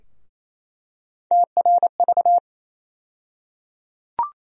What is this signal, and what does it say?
Morse code, heard as a keyed tone, about 28 wpm, 700 Hz. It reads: TRV
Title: USA Call Sign Postfixes - 3 Letters 28wpm